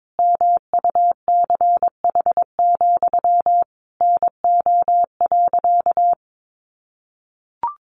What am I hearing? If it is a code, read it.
Morse code, heard as a keyed tone, about 22 wpm, 700 Hz. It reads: MUCH, NO.
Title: A Sound Of Thunder 22wpm